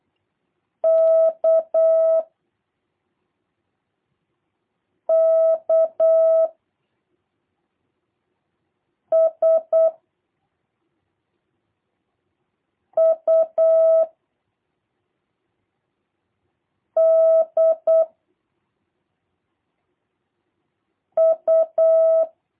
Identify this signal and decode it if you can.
Morse code, heard as a keyed tone, about 8 wpm, 660 Hz. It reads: K K S U D U
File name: jh8cbh / 2023cw / review1-3